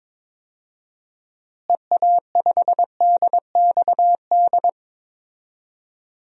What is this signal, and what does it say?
Morse code, heard as a keyed tone, about 22 wpm, 700 Hz. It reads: EA5DXD
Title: DXCC-100 Call Signs - 6 Characters - Repeated 3x - 22wpm